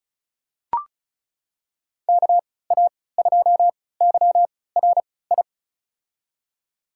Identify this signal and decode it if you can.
Morse code, heard as a keyed tone, about 36 wpm, spaced as for 20 wpm, 700 Hz. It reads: KA2YRI
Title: USA Call Signs - 6 Characters - 3x Character Spacing - 35wpm